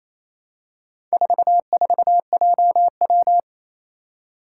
Morse code, heard as a keyed tone, about 27 wpm, 700 Hz. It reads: 44JW
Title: Four Challenging Letter-Number Combinations 28wpm